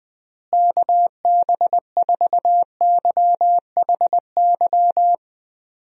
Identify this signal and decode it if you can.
Morse code, heard as a keyed tone, about 20 wpm, 700 Hz. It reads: KB4YHY
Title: USA Call Signs - 6 Characters - Repeated 3x - 20wpm